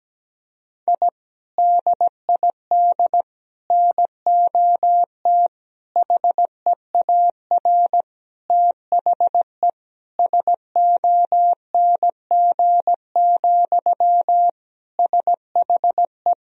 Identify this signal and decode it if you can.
Morse code, heard as a keyed tone, about 17 wpm, 700 Hz. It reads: I DID NOT HEAR THE SONG, SHE
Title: Sentences from Top 1000 Words - Review Part 2 - 17wpm